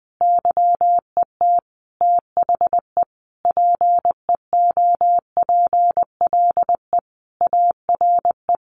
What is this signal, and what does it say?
Morse code, heard as a keyed tone, about 20 wpm, 700 Hz. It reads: YET THE PEOPLE ARE
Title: Sentences from Top 1100 Words 20wpm